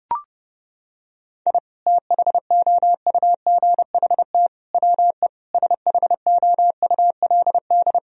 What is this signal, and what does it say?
Morse code, heard as a keyed tone, about 30 wpm, 700 Hz. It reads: I THOUGHT WE SHOULD
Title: Sentences from Top 1000 Words - Review Part 2 - 30wpm